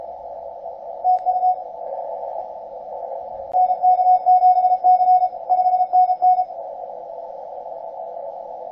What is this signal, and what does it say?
Morse code, heard as a keyed tone, about 33 wpm, 720 Hz. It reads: TU NK7C 5NN